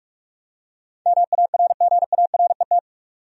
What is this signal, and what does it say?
Morse code, heard as a keyed tone, about 46 wpm, 700 Hz. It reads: MARGARET
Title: USA Common Names - 45wpm